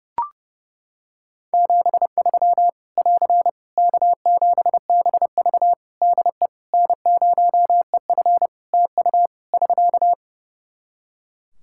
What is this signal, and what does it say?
Morse code, heard as a keyed tone, about 32 wpm, 700 Hz. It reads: 73 <AR> K7BV DE N0EF TU <SK>
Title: Practice QSO - Exchange 4 - Ending - 30wpm